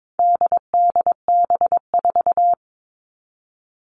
Morse code, heard as a keyed tone, about 22 wpm, 700 Hz. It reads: DDB4
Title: All Cut Numbers - 4-Digit 22wpm